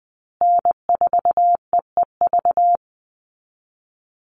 Morse code, heard as a keyed tone, about 20 wpm, 700 Hz. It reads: N4EEV
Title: Call Signs Repeated 3x - 20wpm